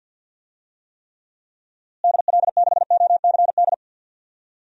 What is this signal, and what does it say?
Morse code, heard as a keyed tone, about 48 wpm, 700 Hz. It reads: DL6OCD
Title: DXCC-100 Call Signs - 6 Characters - Repeated 3x - 50wpm